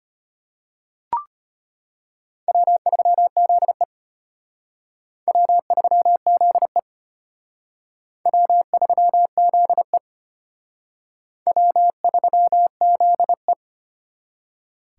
Morse code, heard as a keyed tone, about 33 wpm, 700 Hz. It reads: W3ZE W3ZE W3ZE W3ZE